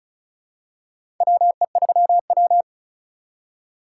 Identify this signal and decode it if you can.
Morse code, heard as a keyed tone, about 34 wpm, 700 Hz. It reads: WE3W
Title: USA Call Signs - 6x Character Spacing - 35wpm